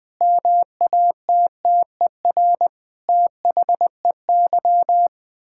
Morse code, heard as a keyed tone, about 20 wpm, 700 Hz. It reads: MATTER THEY
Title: Sentences with 7-Word Phrases - 20wpm